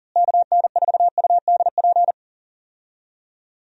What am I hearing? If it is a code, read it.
Morse code, heard as a keyed tone, about 39 wpm, 700 Hz. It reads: KN4UDP
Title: USA Call Signs - 6 Characters - Repeated 3x - 40wpm